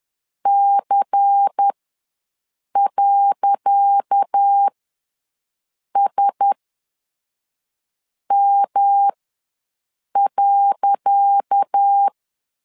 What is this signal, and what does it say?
Morse code, heard as a keyed tone, about 10 wpm, spaced as for 5 wpm, 800 Hz. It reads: C.SM.
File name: LEZIONI / 10 SIGNS / 10C DOT ws 10wpm 4213